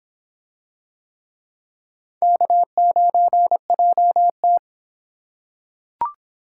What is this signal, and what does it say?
Morse code, heard as a keyed tone, about 26 wpm, 700 Hz. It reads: K9JT